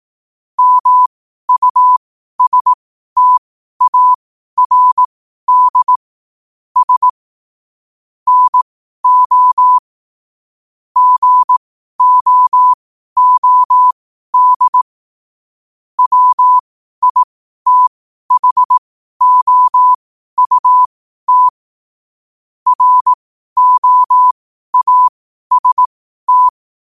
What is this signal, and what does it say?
Morse code, heard as a keyed tone, about 18 wpm, spaced as for 12 wpm, 1000 Hz. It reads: MUSTARD S NO GOOD WITHOUT ROAST
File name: qotdfiles / 12wpm